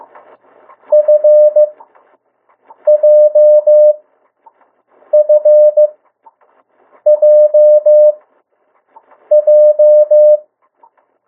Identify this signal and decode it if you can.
Morse code, heard as a keyed tone, about 15 wpm, spaced as for 6 wpm, 600 Hz. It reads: FJFJJ
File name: morse mp3 / 02 urok 1